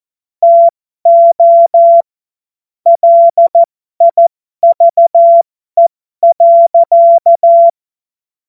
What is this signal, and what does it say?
Morse code, heard as a keyed tone, about 14 wpm, spaced as for 12 wpm, 680 Hz. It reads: TO LIVE.
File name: Morsekurs / Witze